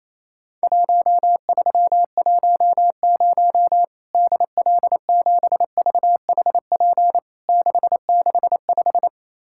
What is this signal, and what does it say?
Morse code, heard as a keyed tone, about 28 wpm, 700 Hz. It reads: 1310 DL7VHP 665
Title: Worked All Europe DX Contest - QTC Exchange - 28wpm